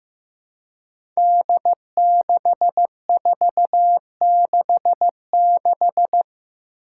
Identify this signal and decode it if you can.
Morse code, heard as a keyed tone, about 15 wpm, 700 Hz. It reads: D6466
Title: Five Challenging Letter-Number Combinations 15wpm